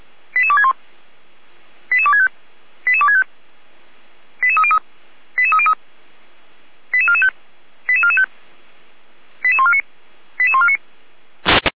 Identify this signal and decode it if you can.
Five-tone selective call (ZVEI), heard as quick runs of five tones, tones of about 70 ms. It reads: 88261 88265 88265 88303 88303 88445 88445 88149 88149